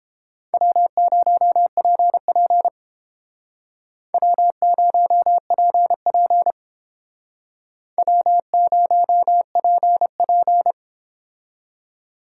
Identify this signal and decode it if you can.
Morse code, heard as a keyed tone, about 30 wpm, 700 Hz. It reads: W0PP W0PP W0PP